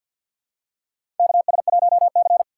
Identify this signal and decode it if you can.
Morse code, heard as a keyed tone, about 52 wpm, 700 Hz. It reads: KS1C
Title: USA Call Signs - 4 Characters - 6x Character Spacing - 50wpm